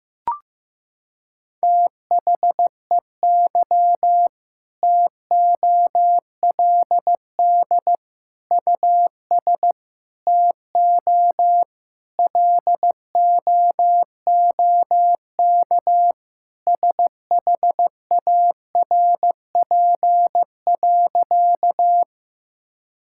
Morse code, heard as a keyed tone, about 15 wpm, 700 Hz. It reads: THEY TOLD US TO LOOK SHARP.
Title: Sentences from Top 1600 Words 15wpm